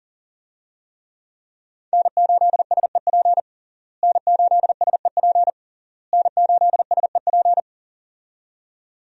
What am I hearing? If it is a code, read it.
Morse code, heard as a keyed tone, about 40 wpm, 700 Hz. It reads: N8SEP N8SEP N8SEP
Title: USA Call Signs - 5 Characters - Repeated 3x - 40wpm